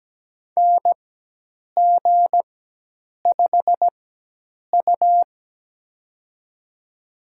Morse code, heard as a keyed tone, about 17 wpm, spaced as for 8 wpm, 700 Hz. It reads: NG5U
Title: USA Call Signs - 4x Character Spacing - 17wpm